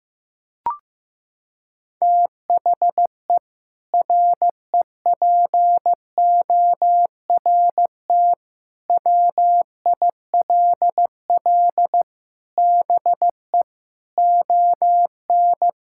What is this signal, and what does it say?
Morse code, heard as a keyed tone, about 16 wpm, 700 Hz. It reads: THE REPORT WILL BE ON
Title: Sentences from Top 1000 Words - Review Part 6 - 15wpm